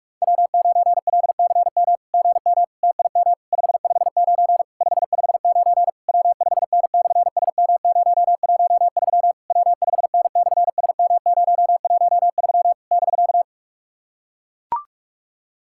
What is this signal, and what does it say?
Morse code, heard as a keyed tone, about 44 wpm, 690 Hz. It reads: W9LCM GM TIM 559 559 W5N/SM013 W5N/SM013 <BK>
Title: SOTA QSO Practice 45wpm